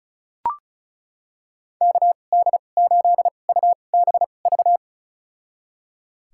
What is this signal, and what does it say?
Morse code, heard as a keyed tone, about 36 wpm, spaced as for 25 wpm, 700 Hz. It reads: KD8UBV
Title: USA Call Signs - 2x Character Spacing - 35wpm